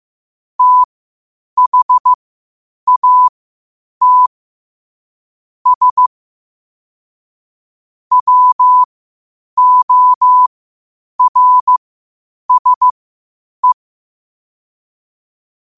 Morse code, heard as a keyed tone, about 15 wpm, spaced as for 8 wpm, 1000 Hz. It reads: THAT S WORSE
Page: Quotes for Fri, 15 Aug 2025 in Morse Code at 8 words per minute.